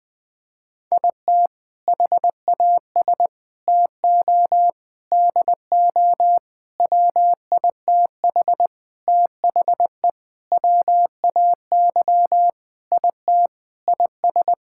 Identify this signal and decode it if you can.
Morse code, heard as a keyed tone, about 20 wpm, 700 Hz. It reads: IT HAS TO DO WITH THE WAY IT IS
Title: Sentences with 5-Word Phrases - Spoken Beginning - 20wpm